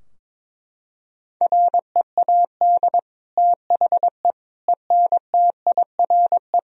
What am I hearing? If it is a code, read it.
Morse code, heard as a keyed tone, about 22 wpm, 700 Hz. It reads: READ THE ENTIRE
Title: Sentences from Top 1000 Words - Review Part 4 - 22wpm